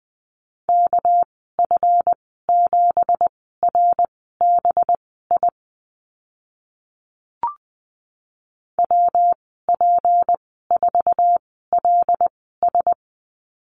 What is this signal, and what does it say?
Morse code, heard as a keyed tone, about 20 wpm, spaced as for 15 wpm, 700 Hz. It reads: KF7RBI WP4LS
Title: 200 Lesson 32g - Letter k - Call Signs 20wpm